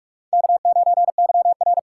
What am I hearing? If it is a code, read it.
Morse code, heard as a keyed tone, about 39 wpm, 700 Hz. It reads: K9YR